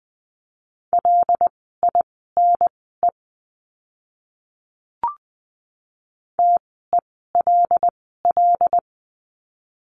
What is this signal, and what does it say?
Morse code, heard as a keyed tone, about 20 wpm, spaced as for 15 wpm, 700 Hz. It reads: LINE TELL
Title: 115 Lesson 19f - Letter w - Words 20wpm